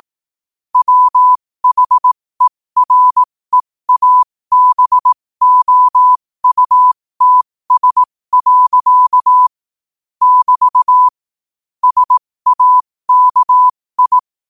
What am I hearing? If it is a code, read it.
Morse code, heard as a keyed tone, about 18 wpm, spaced as for 15 wpm, 1000 Hz. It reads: WHEREABOUTS. <BT> SAKI
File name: qotdfiles / 15wpm